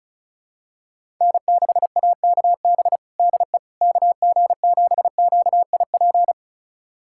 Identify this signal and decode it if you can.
Morse code, heard as a keyed tone, about 36 wpm, 700 Hz. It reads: N6AKB DE KG7QIP
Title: USA Callsign DE Callsign 35wpm